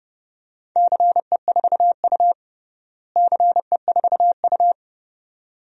Morse code, heard as a keyed tone, about 30 wpm, 700 Hz. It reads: CE4U CE4U
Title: DXCC-100 Call Signs - 4 Characters - Repeated 2x - 30wpm